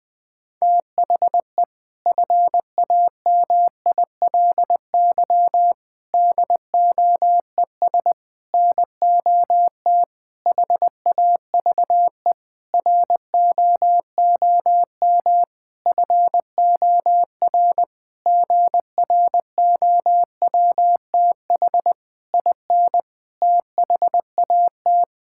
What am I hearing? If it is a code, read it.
Morse code, heard as a keyed tone, about 20 wpm, 700 Hz. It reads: THE FAMILY DOES NOT HAVE ROOM FOR GROWTH IN THAT
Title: Sentences from Top 1000 Words - Review Part 5 - 20wpm